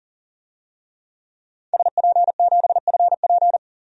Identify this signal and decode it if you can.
Morse code, heard as a keyed tone, about 40 wpm, 700 Hz. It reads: SP7FP
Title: DXCC-100 Call Signs - 5 Characters - 40wpm